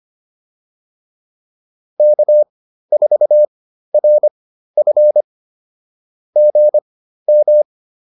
Morse code, heard as a keyed tone, about 25 wpm, spaced as for 13 wpm, 600 Hz. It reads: K4RF GM